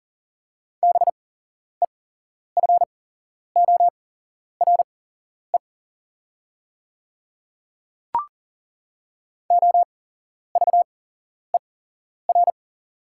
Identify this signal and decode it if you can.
Morse code, heard as a keyed tone, about 40 wpm, spaced as for 11 wpm, 700 Hz. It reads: BEFORE OVER
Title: 8x Character Spacing - Top 200 Words - 40wpm